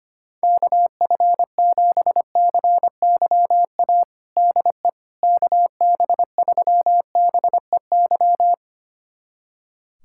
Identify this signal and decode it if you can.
Morse code, heard as a keyed tone, about 25 wpm, 700 Hz. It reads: KF7CYA DE KB3BEY
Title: USA Callsign DE Callsign - 6 Characters - 25wpm